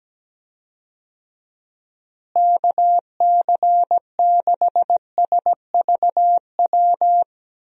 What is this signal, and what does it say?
Morse code, heard as a keyed tone, about 17 wpm, 700 Hz. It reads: KC6SVW
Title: USA Call Signs - 6 Characters - Repeated 2x - 17wpm